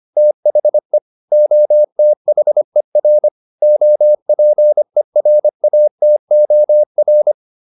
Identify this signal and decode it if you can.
Morse code, heard as a keyed tone, about 25 wpm, 600 Hz. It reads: THE OTHER OPERATOR